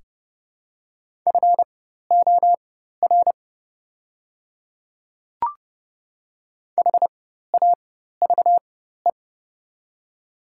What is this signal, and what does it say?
Morse code, heard as a keyed tone, about 30 wpm, spaced as for 14 wpm, 700 Hz. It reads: FOR HAVE
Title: 4x Character Spacing - Top 300 Words - 30wpm